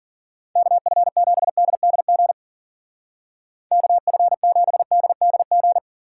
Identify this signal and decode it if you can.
Morse code, heard as a keyed tone, about 48 wpm, 700 Hz. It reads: KF7DDG KF7DDG
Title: Speed Racing - USA 2x3 Call Signs 40wpm